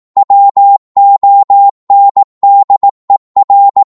Morse code, heard as a keyed tone, about 18 wpm, 800 Hz. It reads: WONDER
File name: Morse / psg-18wpm